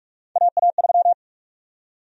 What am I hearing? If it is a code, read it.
Morse code, heard as a keyed tone, about 44 wpm, 700 Hz. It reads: AA3
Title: USA Call Sign Prefixes - 45wpm